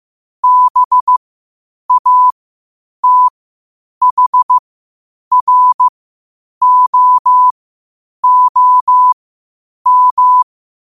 Morse code, heard as a keyed tone, about 15 wpm, spaced as for 8 wpm, 1000 Hz. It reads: BATHROOM
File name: qotdfiles / 8wpm